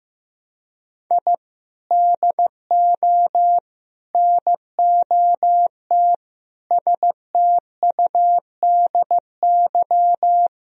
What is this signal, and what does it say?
Morse code, heard as a keyed tone, about 16 wpm, 700 Hz. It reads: I DO NOT STUDY
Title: Sentences from Top 1000 Words - Review Part 1 - 15wpm